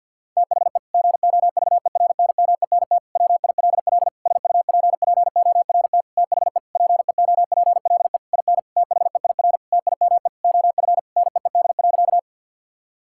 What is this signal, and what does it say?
Morse code, heard as a keyed tone, about 52 wpm, 700 Hz. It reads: THE GOVERNMENT WILL SUPPORT THE PEOPLE IN THEIR TIME OF NEED.
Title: Sentences from Top 1000 Words - Review Part 5 - 50wpm